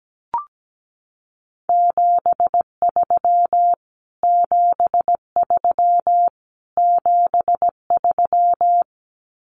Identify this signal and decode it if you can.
Morse code, heard as a keyed tone, about 17 wpm, 700 Hz. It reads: 73 73 73
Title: Abbreviations Repeated 3x - 17wpm